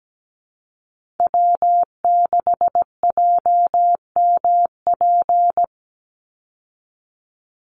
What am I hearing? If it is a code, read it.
Morse code, heard as a keyed tone, about 17 wpm, 700 Hz. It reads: W6JMP
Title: Call Signs - 17wpm